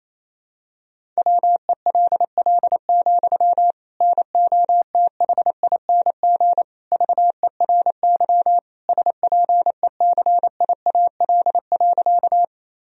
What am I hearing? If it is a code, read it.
Morse code, heard as a keyed tone, about 27 wpm, 700 Hz. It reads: WELL, NOTHING VERY SPECIAL.
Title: Sentences from Top 1000 Words - Review Part 3 - 28wpm